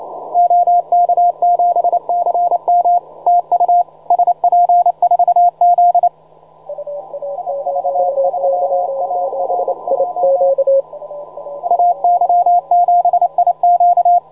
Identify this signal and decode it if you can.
Morse code, heard as a keyed tone, about 28 wpm, 720 Hz, with radio noise. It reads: OK7CM TU SP4Z IV3ARJ AY7IQ
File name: juma-trx2-80m-cw-sample